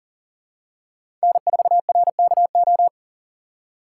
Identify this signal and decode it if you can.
Morse code, heard as a keyed tone, about 40 wpm, 700 Hz. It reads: N4RKO
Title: USA Call Signs - 2x Character Spacing - 40wpm